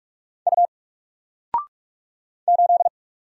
Morse code, heard as a keyed tone, about 44 wpm, 700 Hz. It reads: U 8
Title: Single Character (Letter:Number:Punctuation:Prosign) 45wpm